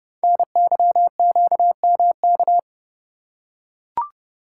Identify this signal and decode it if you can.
Morse code, heard as a keyed tone, about 30 wpm, 700 Hz. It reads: NYQMK
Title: Code Groups - 5 Letters - 30wpm